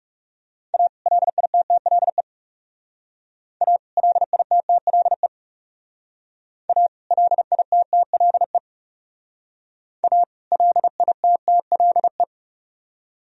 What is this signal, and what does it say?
Morse code, heard as a keyed tone, about 38 wpm, 700 Hz. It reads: A LITTLE A LITTLE A LITTLE A LITTLE